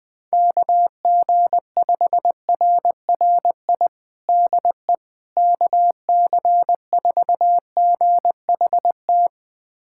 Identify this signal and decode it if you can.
Morse code, heard as a keyed tone, about 21 wpm, 700 Hz. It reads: KG5RRI DE KC4GHT
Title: USA Callsign DE Callsign - 6 Characters - 20wpm